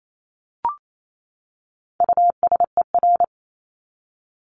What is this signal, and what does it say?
Morse code, heard as a keyed tone, about 28 wpm, 700 Hz. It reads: USER
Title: Top 751-1000 Words 28wpm